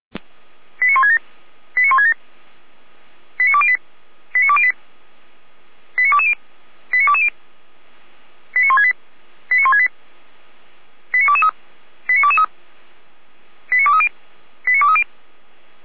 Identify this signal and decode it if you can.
Five-tone selective call (ZVEI), heard as quick runs of five tones, tones of about 70 ms. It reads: 89167 89167 89208 89208 89220 89220 89168 89168 89223 89223 89230 89230